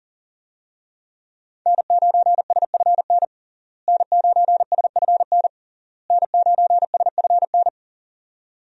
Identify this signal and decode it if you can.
Morse code, heard as a keyed tone, about 40 wpm, 700 Hz. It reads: N9SFN N9SFN N9SFN
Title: USA Call Signs - 5 Characters - Repeated 3x - 40wpm